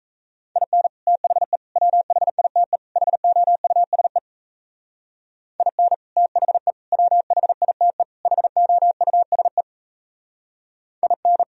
Morse code, heard as a keyed tone, about 33 wpm, 700 Hz. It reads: IN THE WHITE HOUSE IN THE WHITE HOUSE IN
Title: Speed-Racing - Sets of 4 Words 28wpm